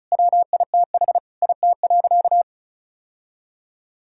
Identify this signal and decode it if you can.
Morse code, heard as a keyed tone, about 36 wpm, 700 Hz. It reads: WITH IT.
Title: Sentences from Top 1000 Words - Review Part 6 - 35wpm